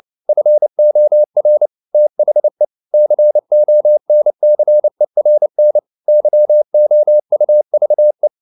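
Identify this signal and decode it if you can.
Morse code, heard as a keyed tone, about 29 wpm, 600 Hz. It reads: FOR THE CONCERN YOUVE
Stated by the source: Morse code practice at 28 wpm, text of Ken Follett's novel Never in English, from CW med Gnister.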